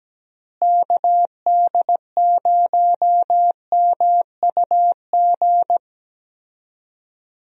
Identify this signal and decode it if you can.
Morse code, heard as a keyed tone, about 17 wpm, 700 Hz. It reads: KD0MUG
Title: USA Call Signs - 6 Characters - Repeated 3x - 17wpm